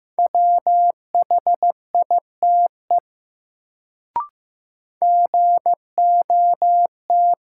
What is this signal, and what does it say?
Morse code, heard as a keyed tone, about 15 wpm, 700 Hz. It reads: WHITE GOT
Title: Sentences from Top 300 Words - New Word Drill - 15wpm